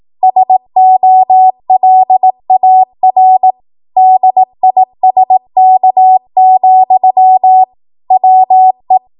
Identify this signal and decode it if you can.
Morse code, heard as a keyed tone, about 18 wpm, 750 Hz. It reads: SOLAR DISK, WE
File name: Morse / 18WPM